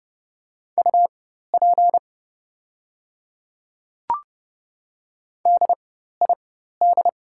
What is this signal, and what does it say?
Morse code, heard as a keyed tone, about 30 wpm, spaced as for 14 wpm, 700 Hz. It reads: UP DID
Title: 4x Character Spacing - Top 100 Words - 30wpm